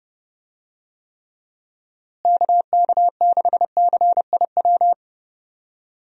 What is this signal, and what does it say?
Morse code, heard as a keyed tone, about 30 wpm, 700 Hz. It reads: KK6CIW
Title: USA Call Signs - 6 Characters - Repeated 2x - 30wpm